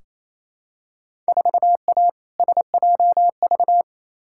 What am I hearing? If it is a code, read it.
Morse code, heard as a keyed tone, about 28 wpm, 700 Hz. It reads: 4A SJV
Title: Field Day Exchanges Encore - 28wpm